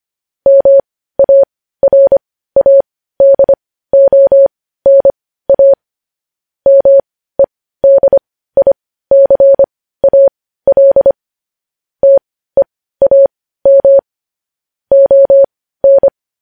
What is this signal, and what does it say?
Morse code, heard as a keyed tone, about 25 wpm, spaced as for 15 wpm, 550 Hz. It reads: MARADONA MEDICAL TEAM ON
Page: News Headlines in Morse Code at 15 WPM